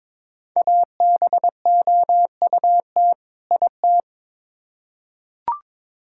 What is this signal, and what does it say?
Morse code, heard as a keyed tone, about 22 wpm, 700 Hz. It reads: ABOUT IT
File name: Build A Phrase - 2-Word Phrases - 22wpm